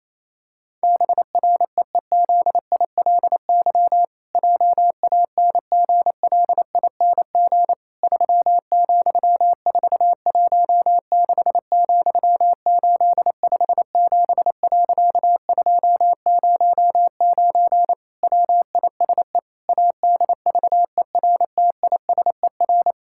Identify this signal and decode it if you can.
Morse code, heard as a keyed tone, about 28 wpm, 700 Hz. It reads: BREEZILY JANGLING 3,416,857.209 WISE ADVERTISER
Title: Pangram Challenge v2 - 28wpm